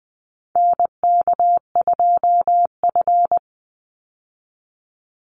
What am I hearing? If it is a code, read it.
Morse code, heard as a keyed tone, about 20 wpm, 700 Hz. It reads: NK2F
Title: Call Signs Repeated 3x - 20wpm